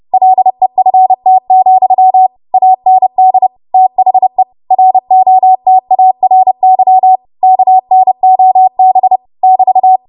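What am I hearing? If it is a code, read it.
Morse code, heard as a keyed tone, about 30 wpm, 750 Hz. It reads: LEFT, AND THE ROTARY KNOB <BT>
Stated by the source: Morse practice transmission at 30 WPM.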